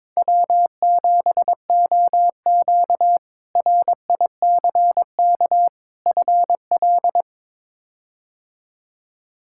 Morse code, heard as a keyed tone, about 22 wpm, 700 Hz. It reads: W7OQ RICK FL
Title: K1USN SST QSO Practice - 22wpm